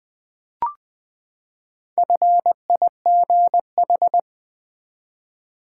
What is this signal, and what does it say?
Morse code, heard as a keyed tone, about 20 wpm, 700 Hz. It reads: FIGH
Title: Build A Word - 4 Letters At A Time - 8-Letter Words - 20wpm